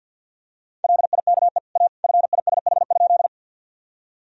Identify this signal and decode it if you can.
Morse code, heard as a keyed tone, about 50 wpm, 700 Hz. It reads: LIKE A FISH?
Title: Sentences from Top 1000 Words - Review Part 1 - 50wpm